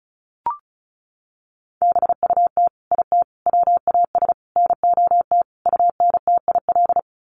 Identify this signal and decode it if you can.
Morse code, heard as a keyed tone, about 34 wpm, 700 Hz. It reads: BUT IT WAS NOT UNTIL
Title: Sets of 5 Words Encore2 35wpm